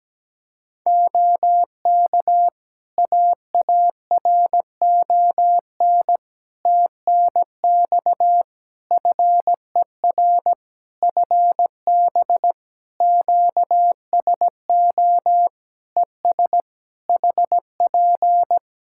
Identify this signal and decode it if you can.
Morse code, heard as a keyed tone, about 17 wpm, 700 Hz. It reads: OK AARON TNX FER FB QSO ES HP
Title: K1USN SST QSO Practice 17wpm